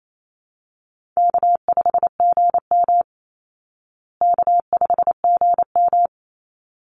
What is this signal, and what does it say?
Morse code, heard as a keyed tone, about 28 wpm, 700 Hz. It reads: K5GM K5GM
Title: Call Signs Repeated 3x - 28wpm